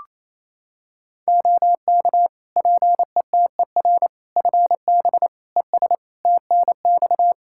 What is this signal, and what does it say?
Morse code, heard as a keyed tone, about 28 wpm, 700 Hz. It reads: OK PETER FB ES TNX
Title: K1USN SST QSO Practice 28wpm